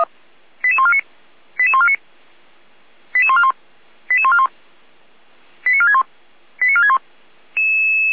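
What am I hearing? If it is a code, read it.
Five-tone selective call (ZVEI), heard as quick runs of five tones, tones of about 70 ms. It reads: 88149 88149 88141 88141 89461 89461
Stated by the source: Five-tone ZVEI selective call received by SDR.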